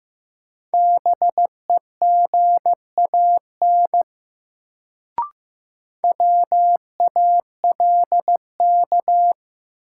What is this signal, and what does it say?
Morse code, heard as a keyed tone, about 15 wpm, 700 Hz. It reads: BEGAN WALK
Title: Sentences from Top 300 Words - New Word Drill - 15wpm